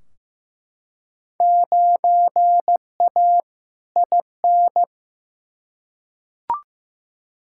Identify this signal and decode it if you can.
Morse code, heard as a keyed tone, about 15 wpm, 700 Hz. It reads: 9A IN
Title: Field Day Exchanges Encore - 15wpm